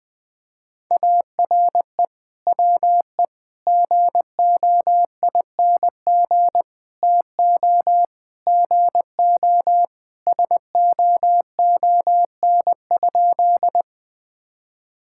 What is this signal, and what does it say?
Morse code, heard as a keyed tone, about 20 wpm, 700 Hz. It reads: ARE WE GOING TO GO SOON?
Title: Sentences with 3-Word Phrases - Spoken Beginning - 20wpm